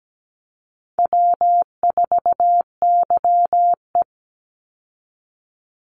Morse code, heard as a keyed tone, about 17 wpm, 700 Hz. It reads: W4YE
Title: Call Signs Repeated 2x - 17wpm